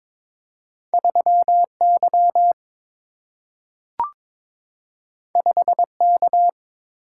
Morse code, heard as a keyed tone, about 22 wpm, 700 Hz. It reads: 3Y 5K
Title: Single-Letter Single-Number 22wpm